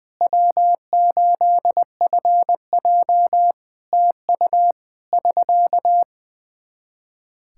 Morse code, heard as a keyed tone, about 20 wpm, 700 Hz. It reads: W8FJ TU <SK>
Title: Practice QSO - Exchange 4 - Ending - 20wpm